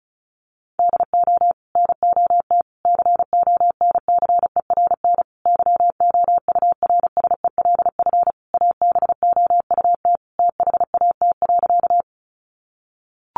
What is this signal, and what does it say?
Morse code, heard as a keyed tone, about 34 wpm, 700 Hz. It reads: DO NOT CONCERN YOURSELF ABOUT THAT.
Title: Sentences from Top 1000 Words 35wpm